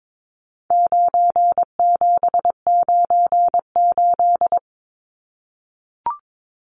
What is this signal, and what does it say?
Morse code, heard as a keyed tone, about 22 wpm, 700 Hz. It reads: 9798
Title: Numbers - 4-Digit 22wpm